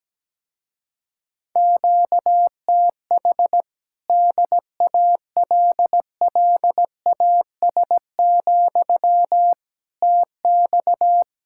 Morse code, heard as a keyed tone, about 18 wpm, 700 Hz. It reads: QTH DALLAS, TX
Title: Speed-Racing Common QSO Phrases - 17wpm